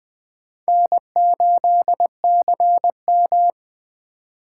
Morse code, heard as a keyed tone, about 20 wpm, 700 Hz. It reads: N8CM
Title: Speed Racing - USA 1x2 Call Signs 17wpm